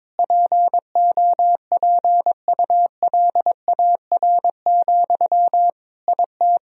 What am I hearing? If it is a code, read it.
Morse code, heard as a keyed tone, about 22 wpm, 700 Hz. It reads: POPULAR, IT
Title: Sentences from Top 1000 Words - Review Part 3 - 22wpm